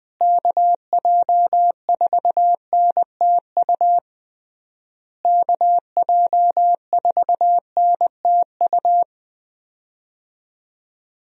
Morse code, heard as a keyed tone, about 20 wpm, 700 Hz. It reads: KJ4NTU KJ4NTU
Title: USA Call Signs - 6 Characters - Repeated 3x - 20wpm